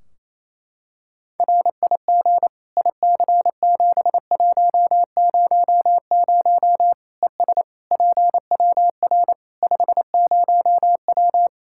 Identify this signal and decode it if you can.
Morse code, heard as a keyed tone, about 28 wpm, 700 Hz. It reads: RIG IC7100 ES PWR 50W
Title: Practice QSO - Exchange 2 - Info - 28wpm